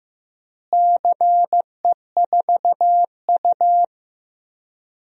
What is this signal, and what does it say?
Morse code, heard as a keyed tone, about 15 wpm, 700 Hz. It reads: CE4U
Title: DXCC-100 Call Signs - 4 Characters - Repeated 2x - 15wpm